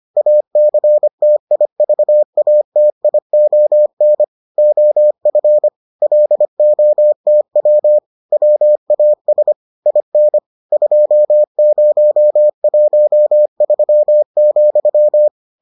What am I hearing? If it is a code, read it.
Morse code, heard as a keyed tone, about 25 wpm, 600 Hz. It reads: ACTIVATION OF LOTW WAS IN 2013,